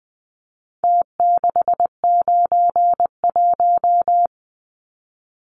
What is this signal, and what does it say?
Morse code, heard as a keyed tone, about 20 wpm, 700 Hz. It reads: T691
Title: Numbers - Contest Style - 4-Digit 20wpm